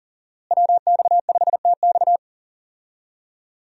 Morse code, heard as a keyed tone, about 37 wpm, 700 Hz. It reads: WX5TX
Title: USA Call Signs - 5 Characters - Repeated 3x - 40wpm